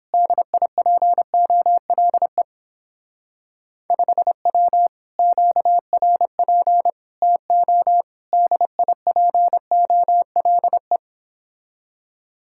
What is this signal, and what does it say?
Morse code, heard as a keyed tone, about 29 wpm, 700 Hz. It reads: DIPOLE 5W QRP TO DIPOLE